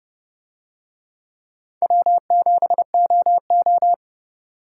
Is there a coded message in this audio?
Yes: Morse code from 1.8 to 3.9 s, heard as a keyed tone, about 29 wpm, 700 Hz. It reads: W7OO